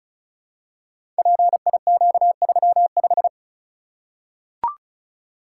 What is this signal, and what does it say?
Morse code, heard as a keyed tone, about 36 wpm, 700 Hz. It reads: PIQ35
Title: Code Groups - 5 Letter:Numbers - 35wpm